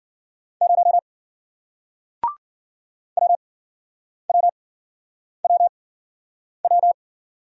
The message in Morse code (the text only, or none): 0 WWWW